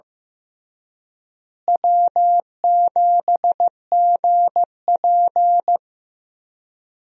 Morse code, heard as a keyed tone, about 15 wpm, 700 Hz. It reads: W7GP
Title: USA Call Signs - 4 Characters - Repeated 3x - 15wpm